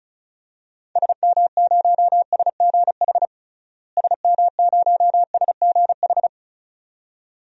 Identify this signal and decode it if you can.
Morse code, heard as a keyed tone, about 34 wpm, 700 Hz. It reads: SM0SGH SM0SGH
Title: DXCC-100 Call Signs - 6 Characters - Repeated 2x - 35wpm